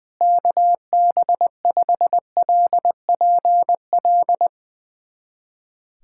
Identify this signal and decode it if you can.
Morse code, heard as a keyed tone, about 20 wpm, 700 Hz. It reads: KB5LPL
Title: USA Callsign DE Callsign - 6 Characters - 20wpm